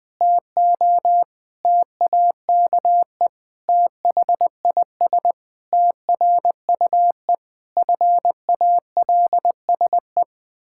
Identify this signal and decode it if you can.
Morse code, heard as a keyed tone, about 20 wpm, 700 Hz. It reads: TO TAKE THIS TRUE FALSE
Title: Pangram Challenge v2 - 20wpm